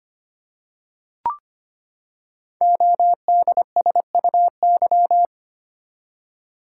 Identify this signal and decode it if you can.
Morse code, heard as a keyed tone, about 25 wpm, 700 Hz. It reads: ODSUY
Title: Code Groups - 5 Letters - 25wpm